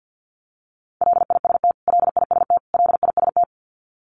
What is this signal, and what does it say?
Morse code, heard as a keyed tone, about 48 wpm, 700 Hz. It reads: LIST LIST LIST
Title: Top 500 Words Repeated 50wpm